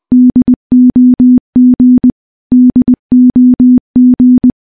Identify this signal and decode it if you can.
Morse code, heard as a keyed tone, about 20 wpm, 250 Hz. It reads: DOG DOG
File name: morse-training-20WPM